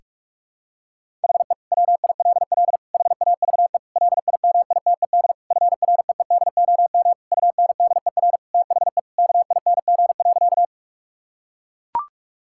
Morse code, heard as a keyed tone, about 42 wpm, 700 Hz. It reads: HE WILL HAVE LIMITED FREEDOM UNDER THE KING.
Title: Sentences from Top 1000 Words - Review Part 6 - 45wpm